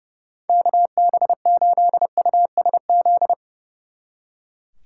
Morse code, heard as a keyed tone, about 39 wpm, 700 Hz. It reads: KB8USZ